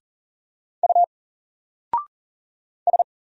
Morse code, heard as a keyed tone, about 40 wpm, 700 Hz. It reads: U S
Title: Single Letter-Number 40wpm